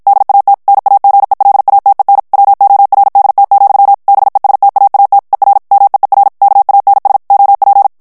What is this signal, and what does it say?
Morse code, heard as a keyed tone, about 52 wpm, 800 Hz. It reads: DAT NAGELATEN WORDT, BESTAAT ER MEER KANS OP
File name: morse mp3 / HOLLANDAIS texte clair 2 vitesse 50 mots minute